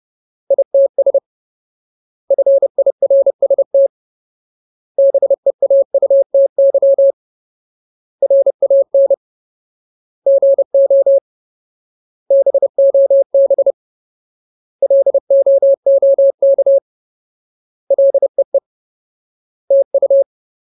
Morse code, heard as a keyed tone, about 30 wpm, 550 Hz. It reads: ITS FIRST BEAUTY RAN GO BOB LOOK LEE TU
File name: Ultimate Word List / 30WPM Ultimate